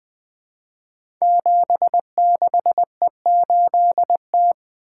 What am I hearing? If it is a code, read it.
Morse code, heard as a keyed tone, about 20 wpm, 700 Hz. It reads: 76E8T
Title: Code Groups - 5 Letter:Numbers - 20wpm